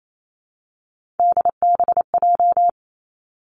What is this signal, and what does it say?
Morse code, heard as a keyed tone, about 28 wpm, 700 Hz. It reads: DBJ
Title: Three Challenging Letter-Number Combinations 28wpm